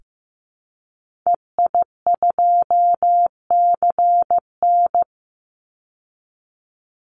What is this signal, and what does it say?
Morse code, heard as a keyed tone, about 15 wpm, 700 Hz. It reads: EI2CN
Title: Call Signs Repeated 3x - 15wpm